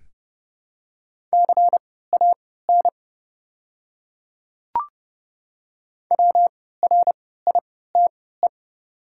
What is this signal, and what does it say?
Morse code, heard as a keyed tone, about 30 wpm, spaced as for 17 wpm, 700 Hz. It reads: CAN WRITE
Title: 3x Character Spacing - Top 100 Words - 30wpm